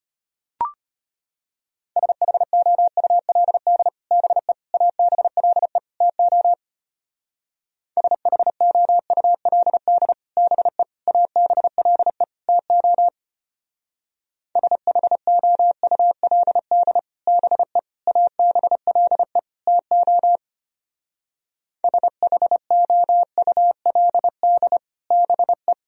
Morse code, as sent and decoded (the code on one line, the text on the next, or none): ... .... --- ..- .-.. -.. / -... / .- -... .-.. / - --- / ... .... --- ..- .-.. -.. / -... / .- -... .-.. / - --- / ... .... --- ..- .-.. -.. / -... / .- -... .-.. / - --- / ... .... --- ..- .-.. -.. / -... .
SHOULD BE ABLE TO SHOULD BE ABLE TO SHOULD BE ABLE TO SHOULD BE